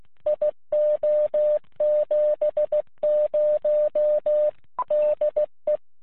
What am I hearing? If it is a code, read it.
Morse code, heard as a keyed tone, about 16 wpm, 600 Hz. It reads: IO70 DE
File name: jh5fis-qso